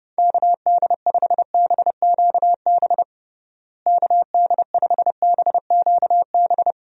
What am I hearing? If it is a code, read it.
Morse code, heard as a keyed tone, about 32 wpm, 700 Hz. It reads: KD5BQB KD5BQB
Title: USA Call Signs - 6 Characters - Repeated 2x - 30wpm